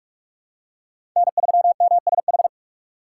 Morse code, heard as a keyed tone, about 42 wpm, 700 Hz. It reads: N3MSH
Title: USA Call Signs - 5 Characters - 2x Character Spacing - 45wpm